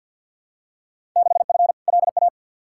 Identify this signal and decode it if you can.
Morse code, heard as a keyed tone, about 49 wpm, 700 Hz. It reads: 6F LA
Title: Field Day Exchanges Encore - 50wpm